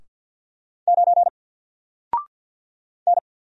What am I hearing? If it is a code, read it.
Morse code, heard as a keyed tone, about 50 wpm, 700 Hz. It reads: 9 N
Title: Single Letter-Number Alt 50wpm